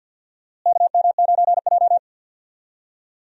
Morse code, heard as a keyed tone, about 50 wpm, 700 Hz. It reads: KM9J
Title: USA Call Signs - 4 Characters - Repeated 3x - 50wpm